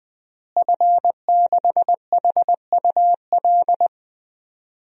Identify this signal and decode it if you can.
Morse code, heard as a keyed tone, about 21 wpm, 700 Hz. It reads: F6HUL
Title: DXCC-100 Call Signs - 5 Characters - Repeated 2x - 20wpm